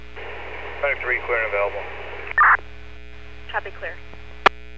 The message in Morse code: none

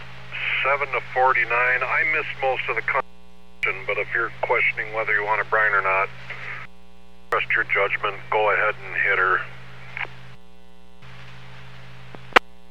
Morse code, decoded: none

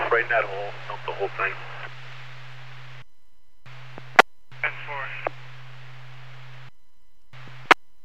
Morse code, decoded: none

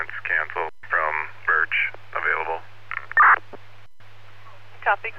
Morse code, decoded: none

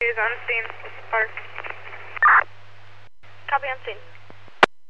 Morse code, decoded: none

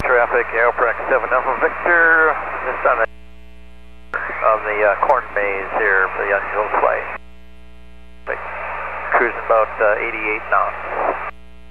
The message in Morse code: none

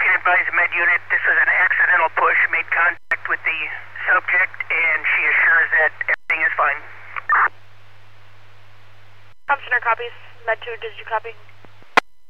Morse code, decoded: none

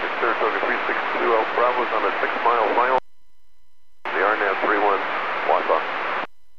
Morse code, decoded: none

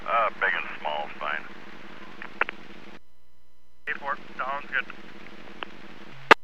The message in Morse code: none

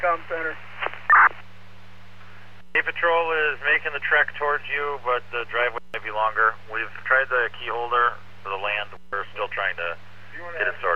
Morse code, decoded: none

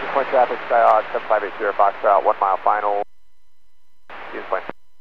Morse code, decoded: none

